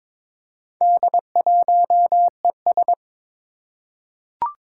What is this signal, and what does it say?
Morse code, heard as a keyed tone, about 22 wpm, 700 Hz. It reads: D1ES
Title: Four Challenging Letter-Number Combinations 22wpm